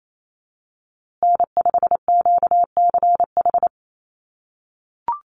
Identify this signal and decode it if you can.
Morse code, heard as a keyed tone, about 28 wpm, 700 Hz. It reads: N5QCH
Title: USA Call Signs Repeated 3x 28wpm